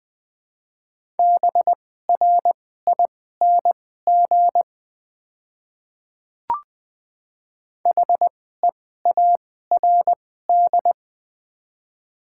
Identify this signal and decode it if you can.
Morse code, heard as a keyed tone, about 20 wpm, spaced as for 14 wpm, 700 Hz. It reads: BRING HEARD
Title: 2x Character Spacing - Top 500 Words - 20wpm